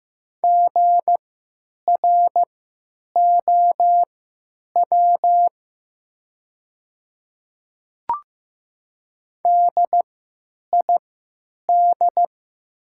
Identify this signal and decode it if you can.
Morse code, heard as a keyed tone, about 15 wpm, spaced as for 9 wpm, 700 Hz. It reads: GROW DID
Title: 3x Character Spacing - Top 300 Words - 15wpm